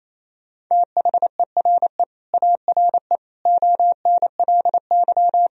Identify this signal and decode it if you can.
Morse code, heard as a keyed tone, about 28 wpm, 700 Hz. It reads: THERE ARE ONLY
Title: Sentences from Top 1000 Words - Review Part 1 - 28wpm